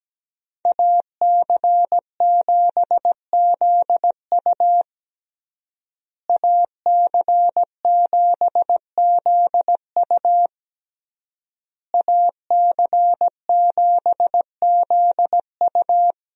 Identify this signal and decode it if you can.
Morse code, heard as a keyed tone, about 17 wpm, 700 Hz. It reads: AC7ZU AC7ZU AC7ZU
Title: USA Call Signs - 5 Characters - Repeated 3x - 17wpm